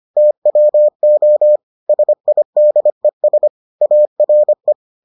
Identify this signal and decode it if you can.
Morse code, heard as a keyed tone, about 25 wpm, 600 Hz. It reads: TWO SIDES ARE